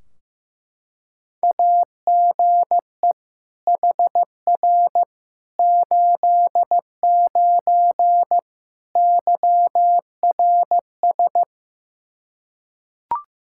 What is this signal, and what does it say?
Morse code, heard as a keyed tone, about 15 wpm, 700 Hz. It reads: AGE HR 89 YRS
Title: Practice QSO - Exchange 3 - More Info - 15wpm